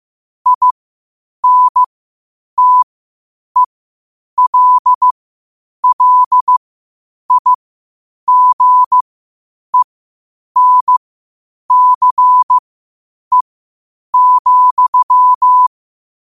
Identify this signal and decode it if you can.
Morse code, heard as a keyed tone, about 15 wpm, spaced as for 8 wpm, 1000 Hz. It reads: INTELLIGENCE,
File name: qotdfiles / 8wpm